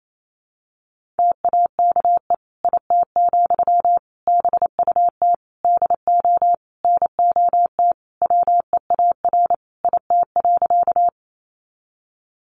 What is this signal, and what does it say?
Morse code, heard as a keyed tone, about 28 wpm, 700 Hz. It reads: TAKE IT, BUT DO NOT WEAR IT.
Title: Sentences from Top 1500 Words 28wpm